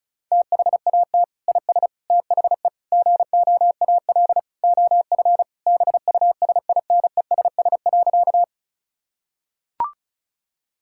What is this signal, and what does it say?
Morse code, heard as a keyed tone, about 34 wpm, 700 Hz. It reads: THAT IS THE GOAL OF BUSINESS.
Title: Sentences from Top 1000 Words - Review Part 5 - 35wpm